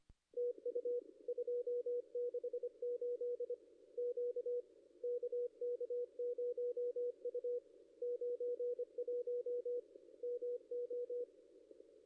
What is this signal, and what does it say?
Morse code, heard as a keyed tone, about 27 wpm, 470 Hz. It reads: TU 268 Q KK0U 91 MO
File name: ss-cw-k5tr-op-n5rz